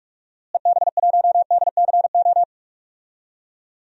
Morse code, heard as a keyed tone, about 46 wpm, 700 Hz. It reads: EB1DCO
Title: DXCC-100 Call Signs - 6 Characters - Repeated 3x - 45wpm